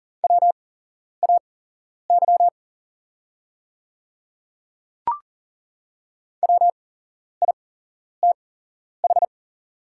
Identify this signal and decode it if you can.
Morse code, heard as a keyed tone, about 40 wpm, spaced as for 11 wpm, 690 Hz. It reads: WAY WITH
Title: 8x Character Spacing - Top 200 Words - 40wpm